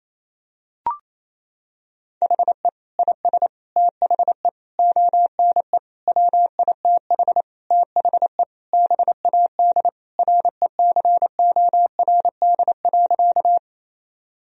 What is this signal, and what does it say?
Morse code, heard as a keyed tone, about 28 wpm, 700 Hz. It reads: HE IS THE ONE WITH THE BAD RECORD.
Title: Sentences from Top 1000 Words - Review Part 3 - 28wpm